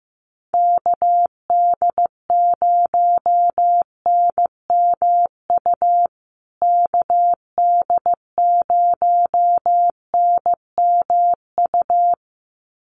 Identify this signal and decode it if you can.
Morse code, heard as a keyed tone, about 15 wpm, 700 Hz. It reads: KD0NMU KD0NMU
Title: USA Call Signs Repeated 2x 15wpm